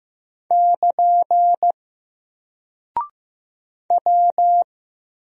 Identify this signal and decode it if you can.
Morse code, heard as a keyed tone, about 15 wpm, 700 Hz. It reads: <KN> W
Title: Single Character (Letter:Number:Punctuation:Prosign) 15wpm